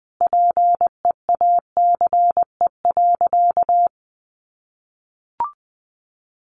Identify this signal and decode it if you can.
Morse code, heard as a keyed tone, about 20 wpm, 700 Hz. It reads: PEACE.
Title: Sentences from Top 1500 Words 20wpm